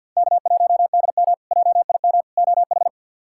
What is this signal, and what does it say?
Morse code, heard as a keyed tone, about 50 wpm, 700 Hz. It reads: K1DM JIM OH
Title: K1USN SST QSO Practice - 50wpm